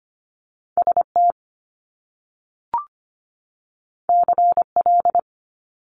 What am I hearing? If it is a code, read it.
Morse code, heard as a keyed tone, about 25 wpm, 700 Hz. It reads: ST CL
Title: Top Two Letter Combinations in Words 25wpm